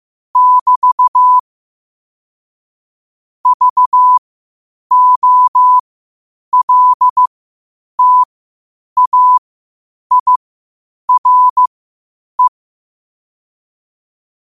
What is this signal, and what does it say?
Morse code, heard as a keyed tone, about 15 wpm, spaced as for 8 wpm, 1000 Hz. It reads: <BT> VOLTAIRE
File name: qotdfiles / 8wpm